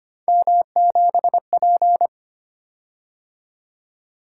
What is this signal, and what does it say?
Morse code, heard as a keyed tone, about 25 wpm, 700 Hz. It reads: M7P
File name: Three Letter-Number Combinations - 25wpm